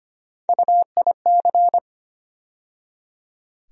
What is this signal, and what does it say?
Morse code, heard as a keyed tone, about 25 wpm, 700 Hz. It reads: UIC
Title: USA Call Sign Postfixes - 3 Letters 25wpm